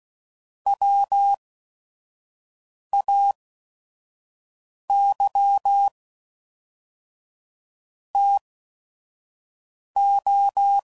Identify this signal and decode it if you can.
Morse code, heard as a keyed tone, about 16 wpm, spaced as for 5 wpm, 800 Hz. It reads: WAY TO